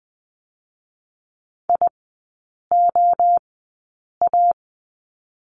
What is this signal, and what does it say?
Morse code, heard as a keyed tone, about 20 wpm, spaced as for 8 wpm, 690 Hz. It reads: IOA
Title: 145 Lesson 24e - Letter y - Three Characters 20wpm